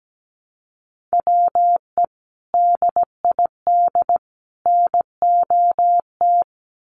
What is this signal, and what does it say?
Morse code, heard as a keyed tone, about 17 wpm, 700 Hz. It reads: WE DID NOT
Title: Sets of 3 Words 17wpm